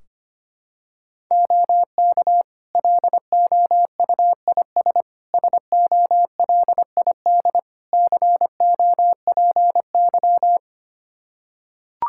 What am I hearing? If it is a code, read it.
Morse code, heard as a keyed tone, about 25 wpm, 700 Hz. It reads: OK LOUIS SOLID COPY
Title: Practice QSO - Exchange 3 - More Info - 25wpm